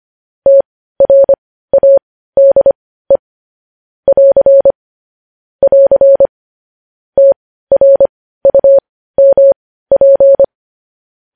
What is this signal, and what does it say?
Morse code, heard as a keyed tone, about 25 wpm, spaced as for 15 wpm, 550 Hz. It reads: TRADE <AR> <AR> TRUMP
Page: news-headlines-in-morse-code-at-15-wpm